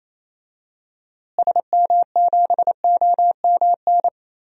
Morse code, heard as a keyed tone, about 28 wpm, 700 Hz. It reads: SM7OMN
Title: DXCC-100 Call Signs - 6 Characters - Repeated 3x - 28wpm